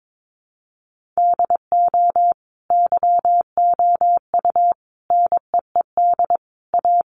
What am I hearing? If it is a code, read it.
Morse code, heard as a keyed tone, about 22 wpm, 700 Hz. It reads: DO YOU NEED A
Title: Sentences from Top 1500 Words 22wpm